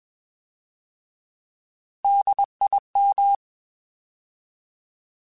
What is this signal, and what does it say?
Morse code, heard as a keyed tone, about 21 wpm, 800 Hz. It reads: DIM
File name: Alphabetical / Alphabetical mp3s / Ds @20